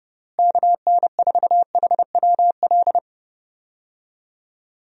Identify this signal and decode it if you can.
Morse code, heard as a keyed tone, about 30 wpm, 700 Hz. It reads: KN4HWL
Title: USA Call Signs - 6 Characters - Repeated 3x - 30wpm